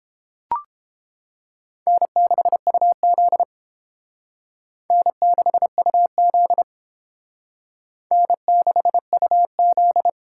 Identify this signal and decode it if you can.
Morse code, heard as a keyed tone, about 28 wpm, 700 Hz. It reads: N6UZ N6UZ N6UZ